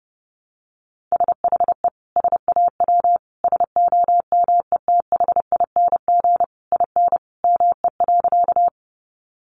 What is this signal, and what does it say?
Morse code, heard as a keyed tone, about 28 wpm, 700 Hz. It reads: SHE SAW SOMETHING IN ME.
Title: Sentences from Top 600 Words 30wpm